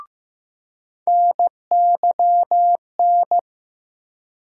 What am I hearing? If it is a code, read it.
Morse code, heard as a keyed tone, about 15 wpm, 700 Hz. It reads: NYN
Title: USA Call Sign Postfixes - 3 Letters 15wpm